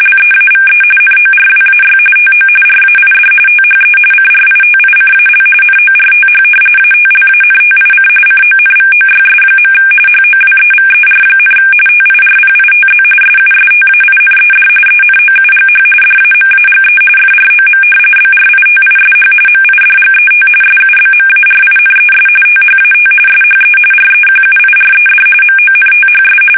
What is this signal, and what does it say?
RTTY, about 45.45 baud. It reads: LEAVE NORWAY TO RAID THE CRITICAL NORTH ATLANTIC CONVOYS. HOWEVER, TIRPITZ SUSTAINED ENOUGH DAMAGE FROM BRITISH MIDGET SUBMARINES THAT SHE REQUIRED MONTHS
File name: fsk850 / battleshipnj